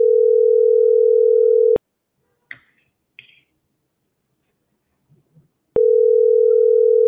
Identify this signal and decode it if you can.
Ringback tone (North America), 440 + 480 Hz.